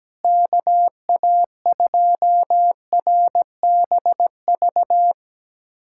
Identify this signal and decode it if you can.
Morse code, heard as a keyed tone, about 17 wpm, 700 Hz. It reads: KA2RBV
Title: USA Callsign DE Callsign - 6 Characters - 17wpm